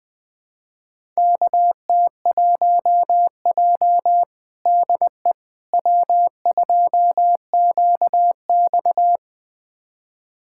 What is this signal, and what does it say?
Morse code, heard as a keyed tone, about 21 wpm, 700 Hz. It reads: KT1J DE W2QX
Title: USA Callsign DE Callsign - 4 Characters - 20wpm